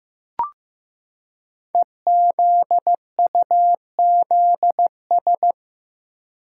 Morse code, heard as a keyed tone, about 15 wpm, 700 Hz. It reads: EZUZS
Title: Code Groups - 5 Letters - 15wpm